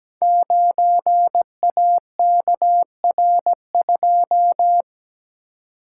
Morse code, heard as a keyed tone, about 17 wpm, 700 Hz. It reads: 9AKR2
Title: Code Groups - 5 Letter:Numbers - 17wpm